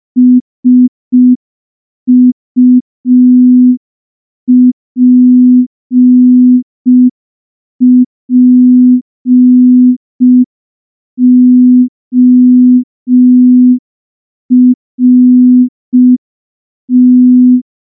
Morse code, heard as a keyed tone, about 5 wpm, 250 Hz. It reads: SUPPORT
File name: morse / morse-training-05WPM